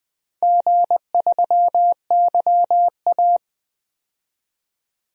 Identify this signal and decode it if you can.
Morse code, heard as a keyed tone, about 20 wpm, 700 Hz. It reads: G3YA
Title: DXCC-100 Call Signs - 4 Characters - 20wpm